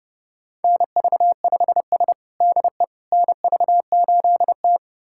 Morse code, heard as a keyed tone, about 30 wpm, 700 Hz. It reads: NV5S DE NV8T
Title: USA Callsign DE Callsign - 4 Characters - 30wpm